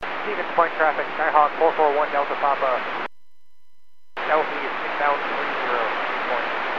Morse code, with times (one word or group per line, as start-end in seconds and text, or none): none